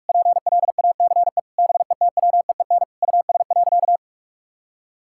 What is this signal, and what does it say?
Morse code, heard as a keyed tone, about 46 wpm, 700 Hz. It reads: PLACE BETWEEN US.
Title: Sentences from Top 1000 Words - Review Part 6 - 45wpm